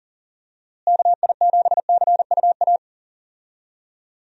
Morse code, heard as a keyed tone, about 37 wpm, 700 Hz. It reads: KI7CUA
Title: USA Call Signs - 6 Characters - Repeated 2x - 40wpm